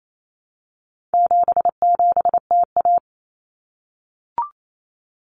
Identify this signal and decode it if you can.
Morse code, heard as a keyed tone, about 28 wpm, 700 Hz. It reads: 77TA
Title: Common Cut Numbers - 4-Digit 28wpm